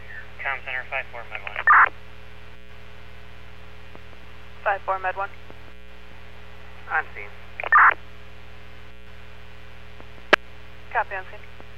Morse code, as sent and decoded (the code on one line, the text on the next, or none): none